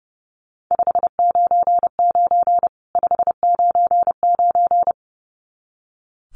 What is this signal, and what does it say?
Morse code, heard as a keyed tone, about 28 wpm, 700 Hz. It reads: 599 599
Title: Common QSO Phrases 30wpm